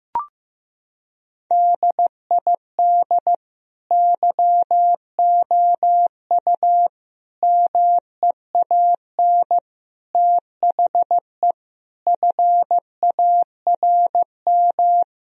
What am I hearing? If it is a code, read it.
Morse code, heard as a keyed tone, about 16 wpm, 700 Hz. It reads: DID YOU MEAN THE FARM
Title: Sentences from Top 1000 Words - Review Part 2 - 15wpm